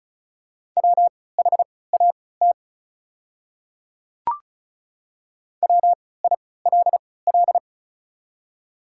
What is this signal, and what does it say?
Morse code, heard as a keyed tone, about 36 wpm, spaced as for 20 wpm, 700 Hz. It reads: WHAT WILL
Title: 3x Character Spacing - Top 200 Words - 35wpm